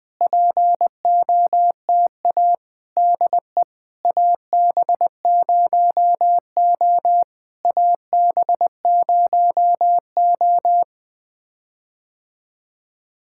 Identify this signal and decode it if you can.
Morse code, heard as a keyed tone, about 20 wpm, 700 Hz. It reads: POTA DE AB0O AB0O
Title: POTA QSO Practice - 20wpm